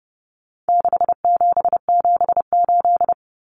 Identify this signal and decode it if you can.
Morse code, heard as a keyed tone, about 30 wpm, 700 Hz. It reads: 6778
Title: Numbers - 4-Digit 30wpm